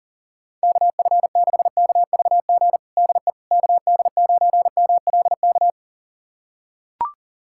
Morse code, heard as a keyed tone, about 40 wpm, 700 Hz. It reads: KF6KVG DE KD9MLK
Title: USA Callsign DE Callsign 40wpm